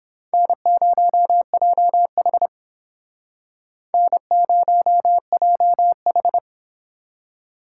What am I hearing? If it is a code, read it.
Morse code, heard as a keyed tone, about 31 wpm, 700 Hz. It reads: N0JH N0JH